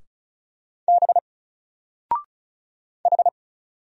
Morse code, heard as a keyed tone, about 35 wpm, 700 Hz. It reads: B H
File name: Single-Letter-Number-35wpm